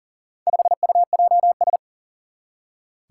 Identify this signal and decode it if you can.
Morse code, heard as a keyed tone, about 40 wpm, 700 Hz. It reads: 5UJS
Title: Four Challenging Letter-Number Combinations 40wpm